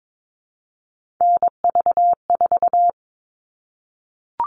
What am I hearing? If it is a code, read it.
Morse code, heard as a keyed tone, about 22 wpm, 700 Hz. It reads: NV4
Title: All Cut Numbers - 3-Digit 22wpm